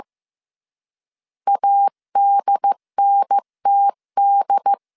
Morse code, heard as a keyed tone, about 15 wpm, 800 Hz. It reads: ADNTD
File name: CW / DET / 1 3 1500C S C 14wpm 2458